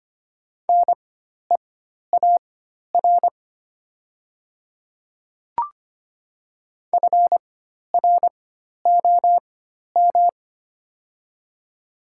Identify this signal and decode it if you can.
Morse code, heard as a keyed tone, about 25 wpm, spaced as for 12 wpm, 700 Hz. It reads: NEAR FROM
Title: 4x Character Spacing - Top 500 Words - 25wpm